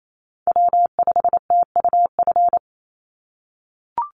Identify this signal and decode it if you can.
Morse code, heard as a keyed tone, about 28 wpm, 700 Hz. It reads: W5TUF
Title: Call Signs - 28wpm